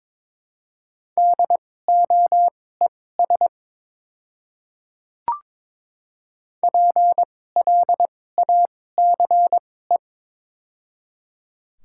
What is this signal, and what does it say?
Morse code, heard as a keyed tone, about 22 wpm, spaced as for 16 wpm, 700 Hz. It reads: DOES PLACE
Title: 2x Character Spacing - Top 300 Words - 22wpm